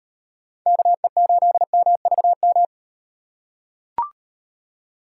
Morse code, heard as a keyed tone, about 35 wpm, 700 Hz. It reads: KE8MVM